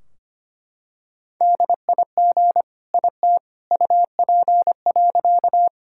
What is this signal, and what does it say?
Morse code, heard as a keyed tone, about 25 wpm, 700 Hz. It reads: DIG IT UP.
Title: Sentences from Top 1600 Words 25wpm